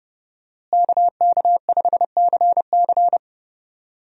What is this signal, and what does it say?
Morse code, heard as a keyed tone, about 31 wpm, 700 Hz. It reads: KK5CC
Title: USA Call Signs - 5 Characters - Repeated 3x - 30wpm